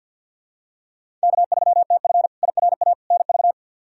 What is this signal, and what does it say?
Morse code, heard as a keyed tone, about 47 wpm, 700 Hz. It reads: K3TF IRA NV
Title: K1USN SST QSO Practice - 50wpm